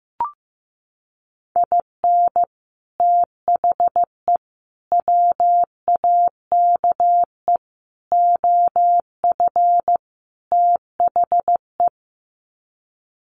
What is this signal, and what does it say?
Morse code, heard as a keyed tone, about 14 wpm, 700 Hz. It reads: IN THE WAKE OF THE
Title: Sets of 5 Words 15wpm